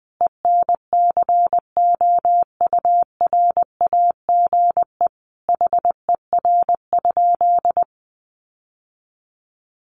Morse code, heard as a keyed tone, about 20 wpm, 700 Hz. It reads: ENCOURAGE HER?
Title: Sentences from Top 1400 Words 20wpm